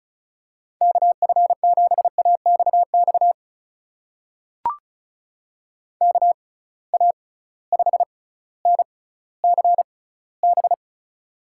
Morse code, heard as a keyed tone, about 34 wpm, spaced as for 18 wpm, 700 Hz. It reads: KF7AXX KA5NCB
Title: USA Call Signs - 6 Characters - 6x Character Spacing - 35wpm